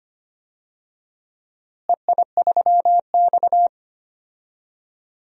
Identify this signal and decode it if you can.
Morse code, heard as a keyed tone, about 25 wpm, 700 Hz. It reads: EI3X
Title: DXCC-100 Call Signs - 4 Characters - Repeated 3x - 25wpm